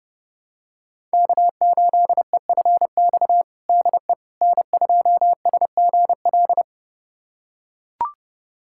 Morse code, heard as a keyed tone, about 32 wpm, 700 Hz. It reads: K8EFX DE N2SGL
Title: USA Callsign DE Callsign - 5 Characters - 30wpm